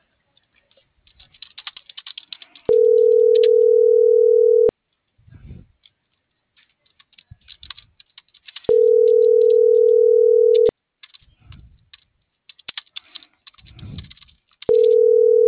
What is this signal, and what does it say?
Ringback tone (North America), 440 + 480 Hz.